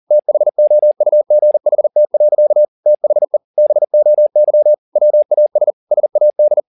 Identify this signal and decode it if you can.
Morse code, heard as a keyed tone, about 40 wpm, 600 Hz. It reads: THOUGHT. THE BOY WAS SAD